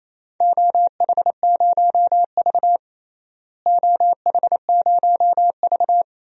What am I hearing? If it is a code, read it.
Morse code, heard as a keyed tone, about 28 wpm, 700 Hz. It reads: OH0V OH0V
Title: DXCC-100 Call Signs - 4 Characters - Repeated 2x - 28wpm